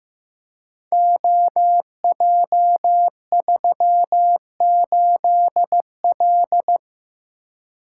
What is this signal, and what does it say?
Morse code, heard as a keyed tone, about 15 wpm, 700 Hz. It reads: OJ38L
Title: Code Groups - 5 Letter:Numbers - 15wpm